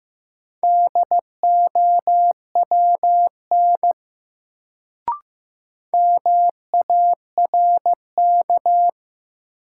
Morse code, heard as a keyed tone, about 15 wpm, 700 Hz. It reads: DOWN MARK
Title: Sentences from Top 300 Words - New Word Drill - 15wpm